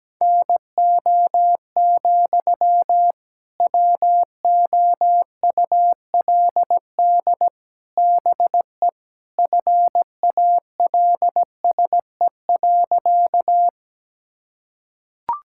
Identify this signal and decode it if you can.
Morse code, heard as a keyed tone, about 17 wpm, 700 Hz. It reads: NO, WOULD BE FALSE.
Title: Sentences from Top 1600 Words 17wpm